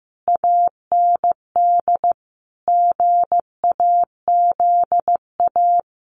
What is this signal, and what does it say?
Morse code, heard as a keyed tone, about 15 wpm, 700 Hz. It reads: AND GAZA
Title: Sets of 5 Words 15wpm